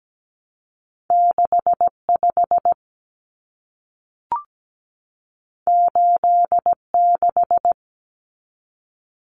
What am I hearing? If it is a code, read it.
Morse code, heard as a keyed tone, about 17 wpm, 700 Hz. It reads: 65 86
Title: Common Cut Numbers - 2-Digit 17wpm